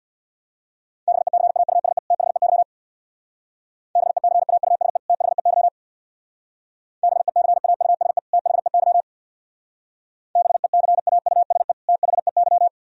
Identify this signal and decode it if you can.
Morse code, heard as a keyed tone, about 61 wpm, 700 Hz. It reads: BECAUSE THEY BECAUSE THEY BECAUSE THEY BECAUSE THEY
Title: Speed-Racing - Sets of 2 Words 50wpm